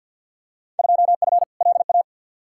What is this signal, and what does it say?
Morse code, heard as a keyed tone, about 51 wpm, 700 Hz. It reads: 2F LA
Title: Field Day Exchanges - 50wpm